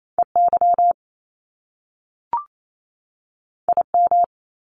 Morse code, heard as a keyed tone, about 28 wpm, 700 Hz. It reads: EY IM
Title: Top Two Letter Combinations in Words 28wpm